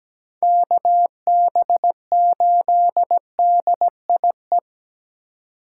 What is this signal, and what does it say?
Morse code, heard as a keyed tone, about 17 wpm, 700 Hz. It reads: KB8DIE
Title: USA Call Signs - 6 Characters - 4x Character Spacing - 17wpm